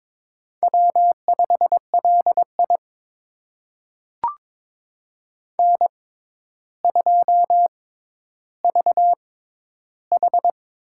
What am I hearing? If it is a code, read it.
Morse code, heard as a keyed tone, about 22 wpm, spaced as for 11 wpm, 700 Hz. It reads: W5LI N2VH
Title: USA Call Signs - 4 Characters - 6x Character Spacing - 22wpm